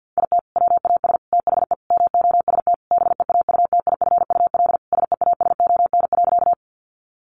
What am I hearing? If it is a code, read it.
Morse code, heard as a keyed tone, about 50 wpm, 700 Hz. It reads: IT WAS THE MOST BEAUTIFUL SEASON.
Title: Sentences from Top 1400 Words 50wpm